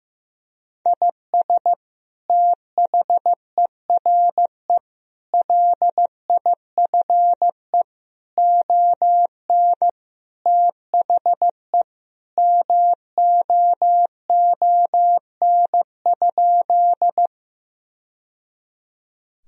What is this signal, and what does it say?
Morse code, heard as a keyed tone, about 15 wpm, 700 Hz. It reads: IS THERE LIFE ON THE MOON?
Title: Sentences from Top 1000 Words - Review Part 3 - 15wpm